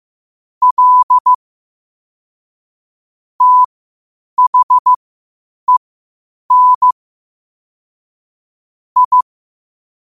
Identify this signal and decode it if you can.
Morse code, heard as a keyed tone, about 15 wpm, spaced as for 8 wpm, 1000 Hz. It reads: L THEN I
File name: qotdfiles / 8wpm